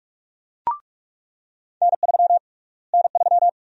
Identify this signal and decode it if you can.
Morse code, heard as a keyed tone, about 44 wpm, 700 Hz. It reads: N3 N3
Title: USA Call Sign Prefixes - Repeated 2x - 45wpm